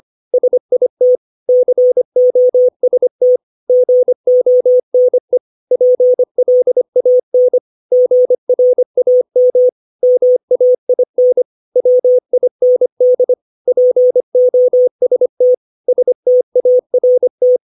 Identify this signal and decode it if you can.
Morse code, heard as a keyed tone, about 25 wpm, 500 Hz. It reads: SIT COST GONE PLAN GRAM MAIN WIND POST START
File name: 632 words / 632 Words 25WPM